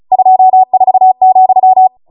Morse code, heard as a keyed tone, about 36 wpm, 750 Hz. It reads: 24,